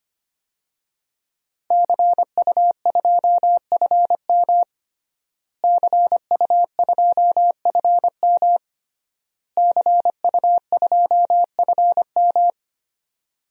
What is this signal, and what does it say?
Morse code, heard as a keyed tone, about 25 wpm, 700 Hz. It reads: CU2FM CU2FM CU2FM
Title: DXCC-100 Call Signs - 5 Characters - Repeated 3x - 25wpm